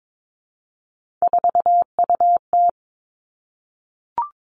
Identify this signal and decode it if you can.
Morse code, heard as a keyed tone, about 22 wpm, 700 Hz. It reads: 4UT
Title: Three Challenging Letter-Number Combinations 22wpm